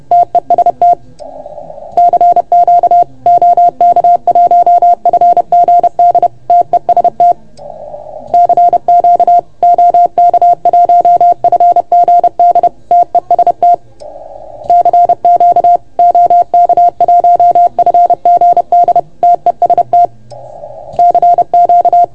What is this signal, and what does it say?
Morse code, heard as a keyed tone, about 30 wpm, 690 Hz. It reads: TEST CQ OK1FGD TEST CQ OK1FGD TEST CQ OK1FGD TEST CQ
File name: Archiv / CUC381 / OK1FGD CUC 381